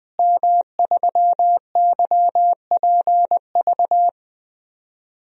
Morse code, heard as a keyed tone, about 20 wpm, 700 Hz. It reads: M3YPV
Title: DXCC-100 Call Signs - 5 Characters - Repeated 2x - 20wpm